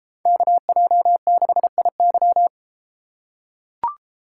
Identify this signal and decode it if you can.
Morse code, heard as a keyed tone, about 32 wpm, 700 Hz. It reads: KJ6IY